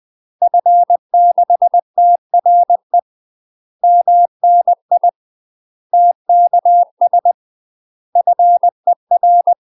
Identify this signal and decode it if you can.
Morse code, heard as a keyed tone, about 20 wpm, 700 Hz. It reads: F6TRE MNI TKS FER
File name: UK MASTER QSOs TRAINING / 20WPM / UK MASTER QSOs 0042